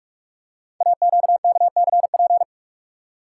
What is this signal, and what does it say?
Morse code, heard as a keyed tone, about 44 wpm, 700 Hz. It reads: AQKCP
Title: Code Groups - 5 Letters - 45wpm